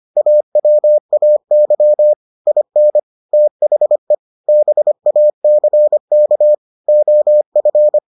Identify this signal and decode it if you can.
Morse code, heard as a keyed tone, about 25 wpm, 600 Hz. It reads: AWAY IN THE BACK OF